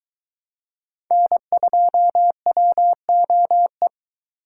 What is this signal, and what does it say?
Morse code, heard as a keyed tone, about 22 wpm, 700 Hz. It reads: N2WOE